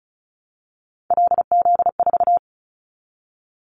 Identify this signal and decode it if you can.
Morse code, heard as a keyed tone, about 34 wpm, 700 Hz. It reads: LZ4
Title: Callsign Prefixes for Top 100 DXCC - 35wpm